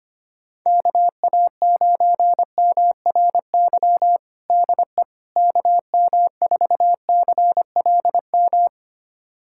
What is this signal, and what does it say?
Morse code, heard as a keyed tone, about 25 wpm, 700 Hz. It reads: KA9MRY DE KM4CLM
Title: USA Callsign DE Callsign - 6 Characters - 25wpm